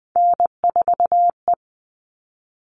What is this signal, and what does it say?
Morse code, heard as a keyed tone, about 20 wpm, 700 Hz. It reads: N4E
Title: All Cut Numbers - 3-Digit 20wpm